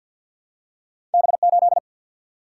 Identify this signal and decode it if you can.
Morse code, heard as a keyed tone, about 54 wpm, 700 Hz. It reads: B8
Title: Single-Letter Single-Number 50wpm